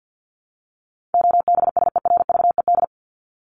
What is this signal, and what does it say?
Morse code, heard as a keyed tone, about 50 wpm, 700 Hz. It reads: OBSERVED
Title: ED Suffix - Top 100 Words with Base Words 50wpm